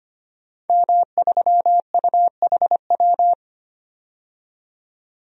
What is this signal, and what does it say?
Morse code, heard as a keyed tone, about 25 wpm, 700 Hz. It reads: M3UHW
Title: DXCC-100 Call Signs - 5 Characters - 25wpm